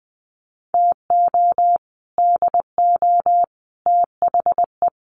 Morse code, heard as a keyed tone, about 20 wpm, 700 Hz. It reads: TO DO THE
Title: Sets of 5 Words Encore 20wpm